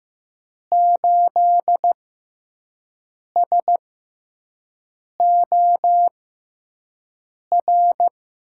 Morse code, heard as a keyed tone, about 15 wpm, spaced as for 5 wpm, 700 Hz. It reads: 8SOR
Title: USA Call Signs - 5 Characters - 6x Character Spacing - 15wpm